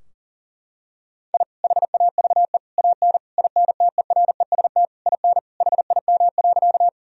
Morse code, heard as a keyed tone, about 40 wpm, 700 Hz. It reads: I HAVE AN INTEREST IN HIM.
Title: Sentences from Top 1000 Words - Review Part 2 - 40wpm